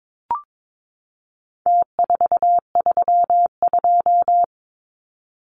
Morse code, heard as a keyed tone, about 22 wpm, 700 Hz. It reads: T432
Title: Numbers - Contest Style - 4-Digit 22wpm